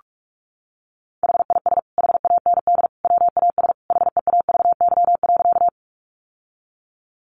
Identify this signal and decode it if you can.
Morse code, heard as a keyed tone, about 44 wpm, 700 Hz. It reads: HIS HAND WAS HEAVY.
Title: Sentences from Top 1100 Words 45wpm